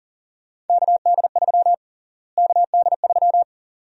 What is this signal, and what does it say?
Morse code, heard as a keyed tone, about 40 wpm, 700 Hz. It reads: KD3 KD3
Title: USA Call Sign Prefixes - Repeated 2x - 40wpm